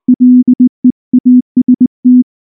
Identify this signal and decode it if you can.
Morse code, heard as a keyed tone, about 18 wpm, 250 Hz. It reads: LEAST
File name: morse / morse-training-20WPM-2